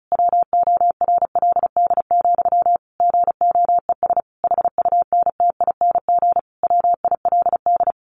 Morse code, heard as a keyed tone, about 34 wpm, 700 Hz. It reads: WORLD, GOES HUNTING WILD
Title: A Sound Of Thunder 35wpm